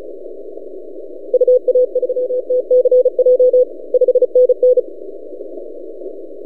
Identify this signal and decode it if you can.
Morse code, heard as a keyed tone, about 34 wpm, 500 Hz. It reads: UA3TCJ 5NN
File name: posts / c5x-30cw- -0600z